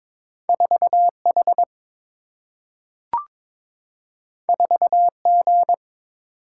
Morse code, heard as a keyed tone, about 22 wpm, 700 Hz. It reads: 4H 4G
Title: Single-Letter Single-Number 22wpm